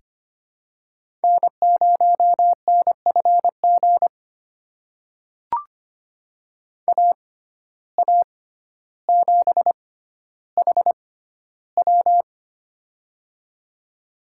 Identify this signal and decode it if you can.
Morse code, heard as a keyed tone, about 25 wpm, spaced as for 13 wpm, 700 Hz. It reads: N0NFG AA7HW
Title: USA Call Signs - 5 Characters - 6x Character Spacing - 25wpm